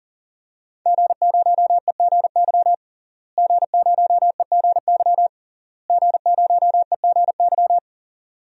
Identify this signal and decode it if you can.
Morse code, heard as a keyed tone, about 40 wpm, 700 Hz. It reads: G0EGY G0EGY G0EGY
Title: DXCC-100 Call Signs - 5 Characters - Repeated 3x - 40wpm